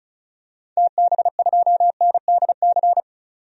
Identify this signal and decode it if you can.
Morse code, heard as a keyed tone, about 34 wpm, 700 Hz. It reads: TB2NDC
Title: DXCC-100 Call Signs - 6 Characters - 35wpm